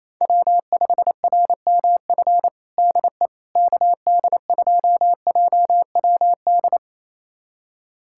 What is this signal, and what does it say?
Morse code, heard as a keyed tone, about 27 wpm, 700 Hz. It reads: W5RMF DE KD2JWD
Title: USA Callsign DE Callsign 28wpm